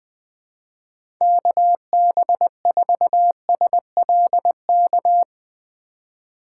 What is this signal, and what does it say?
Morse code, heard as a keyed tone, about 21 wpm, 700 Hz. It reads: KB4SLK
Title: USA Call Signs - 6 Characters - 4x Character Spacing - 20wpm